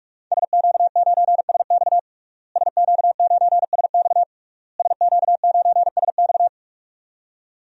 Morse code, heard as a keyed tone, about 46 wpm, 700 Hz. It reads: SQ9SX SQ9SX SQ9SX
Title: DXCC-100 Call Signs - 5 Characters - Repeated 3x - 45wpm